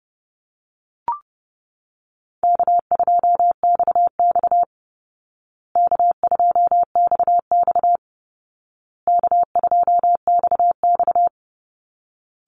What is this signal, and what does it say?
Morse code, heard as a keyed tone, about 30 wpm, 700 Hz. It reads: K2XX K2XX K2XX
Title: Call Signs Repeated 2x - 30wpm